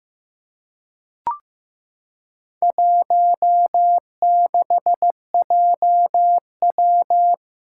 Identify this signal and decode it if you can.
Morse code, heard as a keyed tone, about 15 wpm, 700 Hz. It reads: 16JW
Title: Four Challenging Letter-Number Combinations 15wpm